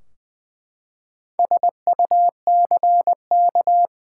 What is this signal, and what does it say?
Morse code, heard as a keyed tone, about 20 wpm, 700 Hz. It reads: SUCK
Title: Four Letter Words - 20wpm